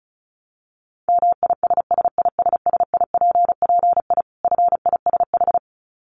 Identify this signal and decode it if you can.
Morse code, heard as a keyed tone, about 34 wpm, 700 Hz. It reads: MISSISSIPPI FISH
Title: Dit Challenge 35wpm